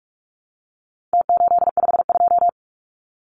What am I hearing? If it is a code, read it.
Morse code, heard as a keyed tone, about 44 wpm, 700 Hz. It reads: T852
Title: Numbers - Contest Style - 4-Digit 45wpm